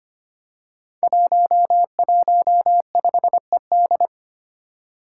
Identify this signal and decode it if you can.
Morse code, heard as a keyed tone, about 25 wpm, 700 Hz. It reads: 115ED
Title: Five Challenging Letter-Number Combinations 25wpm